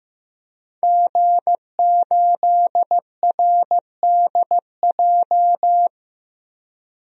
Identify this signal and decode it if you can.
Morse code, heard as a keyed tone, about 15 wpm, 700 Hz. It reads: G8RDJ
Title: DXCC-100 Call Signs - 5 Characters - Repeated 2x - 15wpm